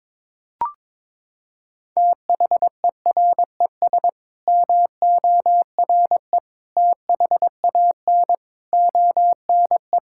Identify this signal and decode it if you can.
Morse code, heard as a keyed tone, about 22 wpm, 700 Hz. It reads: THERES MORE THAN ONE
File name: Sets of 9 Words - 22wpm